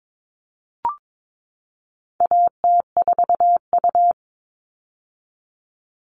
Morse code, heard as a keyed tone, about 22 wpm, 700 Hz. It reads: AT4U
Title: All Cut Numbers - 4-Digit 22wpm